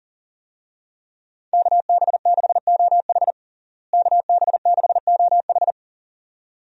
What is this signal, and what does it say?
Morse code, heard as a keyed tone, about 37 wpm, 700 Hz. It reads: KB6OH KB6OH
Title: USA Call Signs - 5 Characters - Repeated 2x - 40wpm